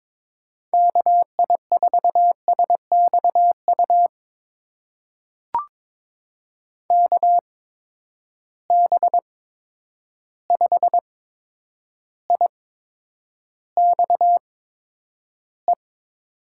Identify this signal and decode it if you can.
Morse code, heard as a keyed tone, about 22 wpm, spaced as for 9 wpm, 700 Hz. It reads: KI4SXU KB5IXE
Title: USA Call Signs - 6 Characters - 8x Character Spacing - 22wpm